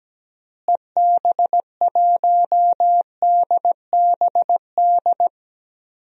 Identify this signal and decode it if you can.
Morse code, heard as a keyed tone, about 17 wpm, 700 Hz. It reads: EB1DBD
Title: DXCC-100 Call Signs - 6 Characters - Repeated 3x - 17wpm